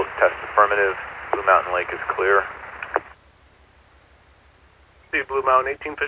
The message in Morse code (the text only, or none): none